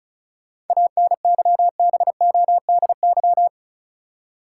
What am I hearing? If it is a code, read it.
Morse code, heard as a keyed tone, about 35 wpm, 700 Hz. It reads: ANYBODY
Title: Seven Letter Words - 35wpm